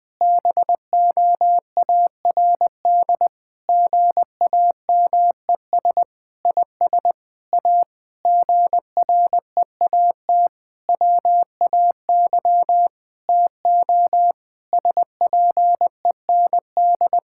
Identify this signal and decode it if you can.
Morse code, heard as a keyed tone, about 20 wpm, 700 Hz. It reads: BOARD GAMES IS A GREAT WAY TO SPEND
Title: Difficult Sentences with Hints - 20wpm